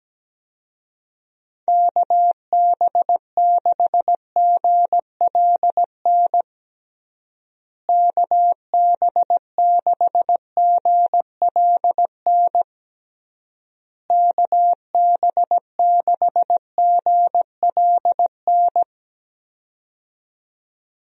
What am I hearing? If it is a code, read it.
Morse code, heard as a keyed tone, about 17 wpm, 700 Hz. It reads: KB6GLN KB6GLN KB6GLN
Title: USA Call Signs - 6 Characters - Repeated 3x - 17wpm